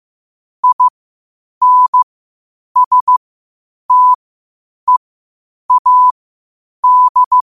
Morse code, heard as a keyed tone, about 15 wpm, spaced as for 8 wpm, 1000 Hz. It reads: INSTEAD
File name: qotdfiles / 8wpm